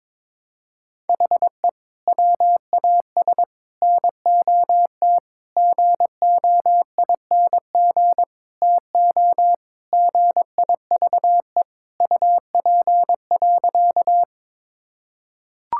Morse code, heard as a keyed tone, about 22 wpm, 700 Hz. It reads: HE WAS NOT GOING TO GIVE UP.
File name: practice / 22wpm / Sentences with 4-Word Phrases - 22wpm